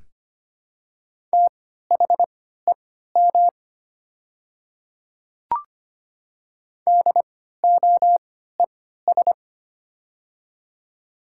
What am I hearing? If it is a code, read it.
Morse code, heard as a keyed tone, about 25 wpm, spaced as for 14 wpm, 700 Hz. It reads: THEM DOES
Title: 3x Character Spacing - Top 200 Words - 25wpm